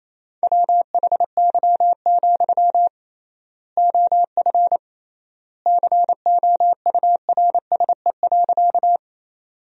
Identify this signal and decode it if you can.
Morse code, heard as a keyed tone, about 28 wpm, 700 Hz. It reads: WHY, OF COURSE.
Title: 3x Word-Spacing - Sentences from Top 500 Words - 28wpm